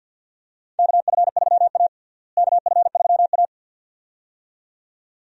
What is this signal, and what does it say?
Morse code, heard as a keyed tone, about 48 wpm, 700 Hz. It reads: KF3A KF3A
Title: USA Call Signs - 4 Characters - Repeated 2x - 50wpm